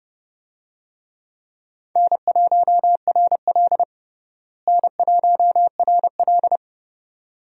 Morse code, heard as a keyed tone, about 31 wpm, 700 Hz. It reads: N1RL N1RL
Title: USA Call Signs - 4 Characters - Repeated 2x - 30wpm